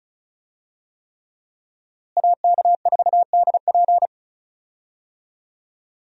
Morse code, heard as a keyed tone, about 34 wpm, 700 Hz. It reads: AK4DP
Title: USA Call Signs - 5 Characters - Repeated 3x - 35wpm